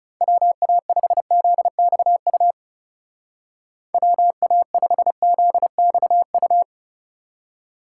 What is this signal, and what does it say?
Morse code, heard as a keyed tone, about 36 wpm, 700 Hz. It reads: WA5ZXU WA5ZXU
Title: Speed Racing - USA 2x3 Call Signs 30wpm